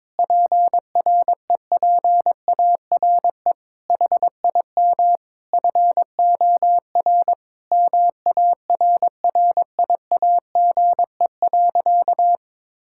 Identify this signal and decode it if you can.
Morse code, heard as a keyed tone, about 22 wpm, 700 Hz. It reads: PREPARE HIM FOR MARRIAGE.
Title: Sentences from Top 1000 Words - Review Part 6 - 22wpm